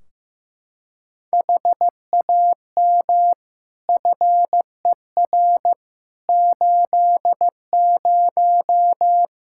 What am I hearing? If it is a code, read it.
Morse code, heard as a keyed tone, about 16 wpm, 700 Hz. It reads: HAM FER 80
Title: Practice QSO - Exchange 3 - More Info - 15wpm